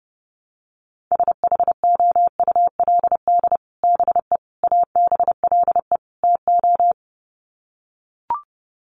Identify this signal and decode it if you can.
Morse code, heard as a keyed tone, about 30 wpm, 700 Hz. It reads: SHOULD BE ABLE TO
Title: Sets of 4 Words 30wpm